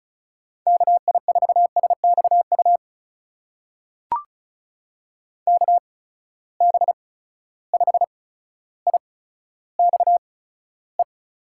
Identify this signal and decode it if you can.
Morse code, heard as a keyed tone, about 34 wpm, spaced as for 15 wpm, 700 Hz. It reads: KI4SXU KB5IXE
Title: USA Call Signs - 6 Characters - 8x Character Spacing - 35wpm